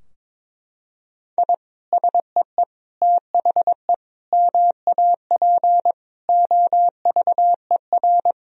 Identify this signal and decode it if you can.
Morse code, heard as a keyed tone, about 22 wpm, 700 Hz. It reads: I SEE THE MAP OVER
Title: Sentences from Top 1000 Words - Review Part 2 - 22wpm